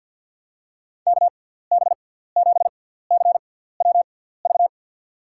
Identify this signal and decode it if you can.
Morse code, heard as a keyed tone, about 52 wpm, spaced as for 17 wpm, 700 Hz. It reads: KB7CWV
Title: USA Call Signs - 6 Characters - 6x Character Spacing - 50wpm